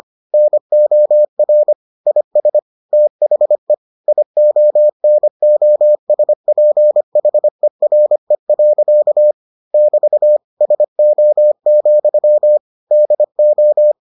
Morse code, heard as a keyed tone, about 25 wpm, 600 Hz. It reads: NOR IS THE IONOSPHERE. <BT> SO, DO